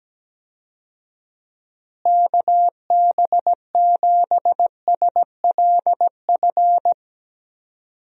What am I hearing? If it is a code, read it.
Morse code, heard as a keyed tone, about 17 wpm, 700 Hz. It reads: KB7SLF
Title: USA Call Signs - 6 Characters - Repeated 2x - 17wpm